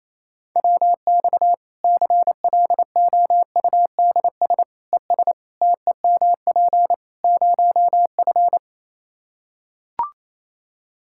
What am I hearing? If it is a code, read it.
Morse code, heard as a keyed tone, about 28 wpm, 700 Hz. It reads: WX CLOUDS ES TEMP 0F
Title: Practice QSO - Exchange 2 - Info - 28wpm